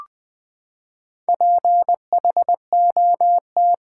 Morse code, heard as a keyed tone, about 20 wpm, 700 Hz. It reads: PHOT
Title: Build A Word - 4 Letters At A Time - 11-Letter Words - 20wpm